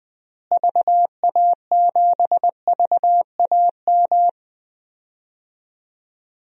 Morse code, heard as a keyed tone, about 20 wpm, 700 Hz. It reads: VA7VAM
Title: DXCC-100 Call Signs - 6 Characters - Repeated 2x - 20wpm